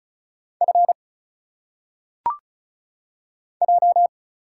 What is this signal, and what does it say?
Morse code, heard as a keyed tone, about 36 wpm, 700 Hz. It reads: F J
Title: Single Letters - Phonetic Pronunciation 35wpm